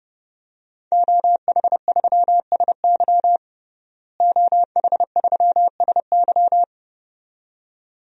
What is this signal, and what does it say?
Morse code, heard as a keyed tone, about 32 wpm, 700 Hz. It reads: OH3SY OH3SY
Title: DXCC-100 Call Signs - 5 Characters - Repeated 2x - 30wpm